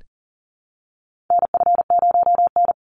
Morse code, heard as a keyed tone, about 40 wpm, 700 Hz. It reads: NF0N
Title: Call Signs Repeated 3x - 40wpm